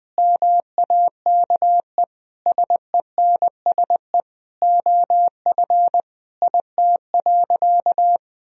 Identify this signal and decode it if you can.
Morse code, heard as a keyed tone, about 20 wpm, 700 Hz. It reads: MAKE SENSE OF IT.
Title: Sentences from Top 1000 Words - Review Part 4 - 20wpm